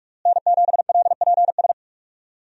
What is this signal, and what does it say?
Morse code, heard as a keyed tone, about 46 wpm, 700 Hz. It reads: N7LPS
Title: USA Call Signs - 5 Characters - Repeated 3x - 45wpm